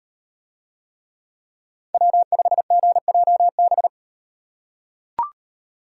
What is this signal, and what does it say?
Morse code, heard as a keyed tone, about 38 wpm, 700 Hz. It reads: W5GJB